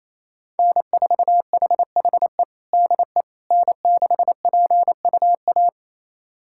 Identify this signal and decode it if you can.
Morse code, heard as a keyed tone, about 27 wpm, 700 Hz. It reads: N4HHE DE N6PUA
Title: USA Callsign DE Callsign - 5 Characters - 28wpm